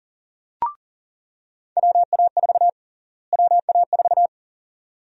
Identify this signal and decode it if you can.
Morse code, heard as a keyed tone, about 40 wpm, 700 Hz. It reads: WA4 WA4
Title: USA Call Sign Prefixes - Repeated 2x - 40wpm